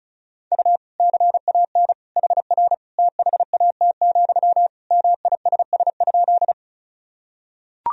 Morse code, heard as a keyed tone, about 36 wpm, 700 Hz. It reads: U CAN HR THAT, MISS?
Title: Sentences from Top 500 Words and Abbreviations 35wpm